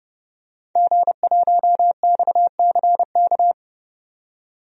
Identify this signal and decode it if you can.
Morse code, heard as a keyed tone, about 31 wpm, 700 Hz. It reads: G1XCK
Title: DXCC-100 Call Signs - 5 Characters - 30wpm